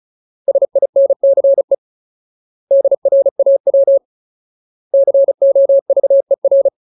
Morse code, heard as a keyed tone, about 34 wpm, 550 Hz. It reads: SINCE DRAW COVER
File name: Ultimate Word List / 35WPM Ultimate